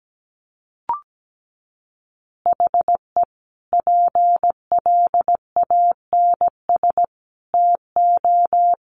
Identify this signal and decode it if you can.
Morse code, heard as a keyed tone, about 17 wpm, 700 Hz. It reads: HE PLANS TO
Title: Additional Easy Phrases 17wpm